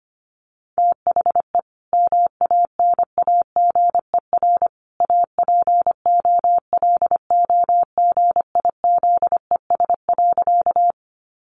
Morse code, heard as a keyed tone, about 25 wpm, 700 Hz. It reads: THE MANAGER APOLOGIZES.
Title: Additional Easy Phrases 25wpm